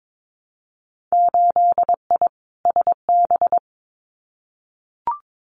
Morse code, heard as a keyed tone, about 22 wpm, 700 Hz. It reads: 8I SB
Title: Winter Field Day Exchanges Encore 22wpm